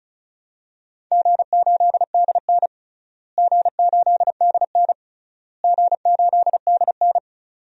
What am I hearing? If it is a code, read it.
Morse code, heard as a keyed tone, about 36 wpm, 700 Hz. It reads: G8DN G8DN G8DN
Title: DXCC-100 Call Signs - 4 Characters - Repeated 3x - 35wpm